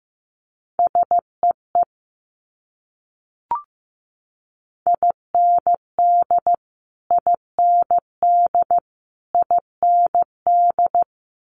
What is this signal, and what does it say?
Morse code, heard as a keyed tone, about 15 wpm, 700 Hz. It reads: SEE IND IND IND
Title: Multi-letter Phonemes Repeated 3x 15wpm